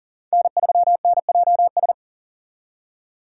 Morse code, heard as a keyed tone, about 50 wpm, 700 Hz. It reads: N3NJS
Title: Speed Racing - USA 1x3 Call Signs 40wpm